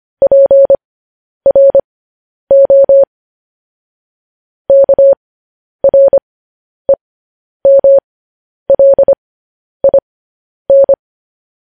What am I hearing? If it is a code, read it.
Morse code, heard as a keyed tone, about 25 wpm, spaced as for 10 wpm, 550 Hz. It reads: PRO KREMLIN